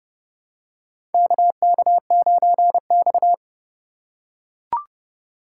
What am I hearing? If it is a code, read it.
Morse code, heard as a keyed tone, about 30 wpm, 700 Hz. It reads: KK9X
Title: USA Call Signs - 4 Characters - 4x Character Spacing - 30wpm